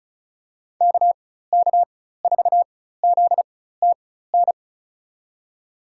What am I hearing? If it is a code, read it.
Morse code, heard as a keyed tone, about 34 wpm, spaced as for 16 wpm, 700 Hz. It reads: KK4ZTN
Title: USA Call Signs - 4x Character Spacing - 35wpm